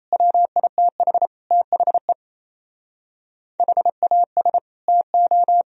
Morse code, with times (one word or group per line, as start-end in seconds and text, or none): 0.1-1.3 WITH
1.5-2.1 THE
3.6-4.6 HAS
4.9-5.6 TO